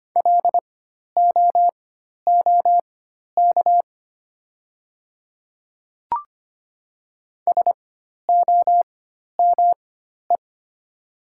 Morse code, heard as a keyed tone, about 25 wpm, spaced as for 12 wpm, 700 Hz. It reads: LOOK SOME
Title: 4x Character Spacing - Top 100 Words - 25wpm